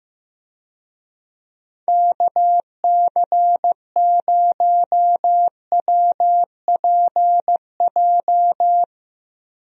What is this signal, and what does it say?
Morse code, heard as a keyed tone, about 15 wpm, 700 Hz. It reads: KC0WPJ
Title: USA Call Signs - 6 Characters - Repeated 2x - 15wpm